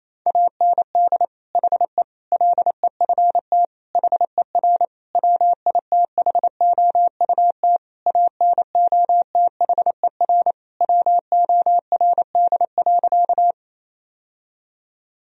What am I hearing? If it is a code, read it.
Morse code, heard as a keyed tone, about 28 wpm, 700 Hz. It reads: AND HE LEFT HER WITHOUT ANOTHER WORD.
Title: Sentences from Top 1000 Words - Review Part 3 - 28wpm